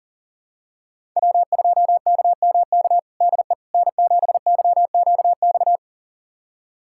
Morse code, heard as a keyed tone, about 40 wpm, 700 Hz. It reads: W2KMK DE N7YQX
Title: USA Callsign DE Callsign - 5 Characters - 40wpm